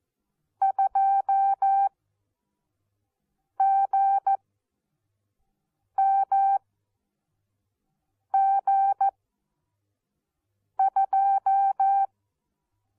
Morse code, heard as a keyed tone, about 14 wpm, spaced as for 4 wpm, 780 Hz. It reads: 2GMG2
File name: Onlinelehrgang / morsen / DJ4UF Morsen14